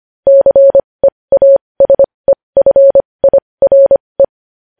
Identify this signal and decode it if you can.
Morse code, heard as a keyed tone, about 25 wpm, spaced as for 20 wpm, 550 Hz. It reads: CEASEFIRE
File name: podcast / News headlines-1x-25x20